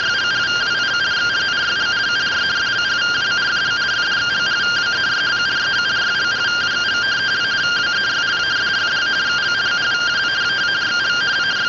Two-tone teleprinter signal, about 170 Hz shift.